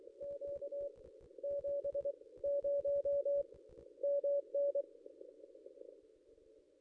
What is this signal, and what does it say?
Morse code, heard as a keyed tone, about 24 wpm, 570 Hz. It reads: Q 7 0 MN
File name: contests / ss-cw-k5tr-op-n5rz -1900